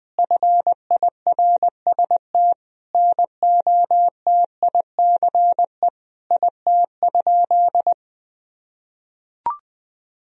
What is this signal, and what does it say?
Morse code, heard as a keyed tone, about 20 wpm, 700 Hz. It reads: FIRST NOTICE IT?
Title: Sentences from Top 1000 Words - Review Part 2 - 20wpm